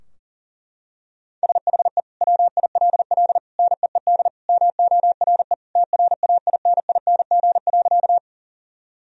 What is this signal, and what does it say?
Morse code, heard as a keyed tone, about 40 wpm, 700 Hz. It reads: SHE WILL NEED MORE TRAINING.
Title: Sentences from Top 1000 Words - Review Part 4 - 40wpm